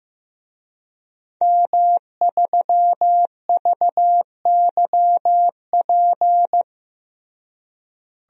Morse code, heard as a keyed tone, about 15 wpm, 700 Hz. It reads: M3VYP
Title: DXCC-100 Call Signs - 5 Characters - Repeated 3x - 15wpm